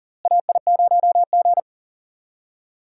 Morse code, heard as a keyed tone, about 39 wpm, 700 Hz. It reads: AI0G
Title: USA Call Signs - 4 Characters - Repeated 3x - 40wpm